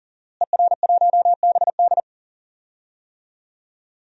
Morse code, heard as a keyed tone, about 37 wpm, 700 Hz. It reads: ER1BD
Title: DXCC-100 Call Signs - 5 Characters - Repeated 3x - 40wpm